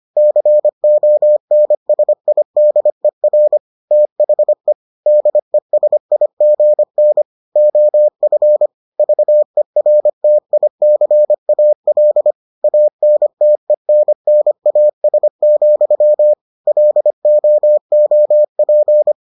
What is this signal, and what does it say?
Morse code, heard as a keyed tone, about 25 wpm, 600 Hz. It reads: CONSIDER THE DESIGN OF VERTICAL ANTENNAS, LOOP